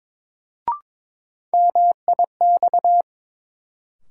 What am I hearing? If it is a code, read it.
Morse code, heard as a keyed tone, about 22 wpm, 700 Hz. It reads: MIX
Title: Sentences from Top 1100 Words - New Word Drill Alt - 22wpm